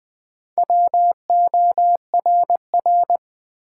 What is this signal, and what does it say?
Morse code, heard as a keyed tone, about 20 wpm, 700 Hz. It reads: WORR
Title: Build A Word - 1 Letter At A Time - 5-Letter Words - 20wpm